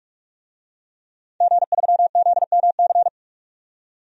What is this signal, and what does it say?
Morse code, heard as a keyed tone, about 46 wpm, 700 Hz. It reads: G3ZMC
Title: DXCC-100 Call Signs - 5 Characters - 45wpm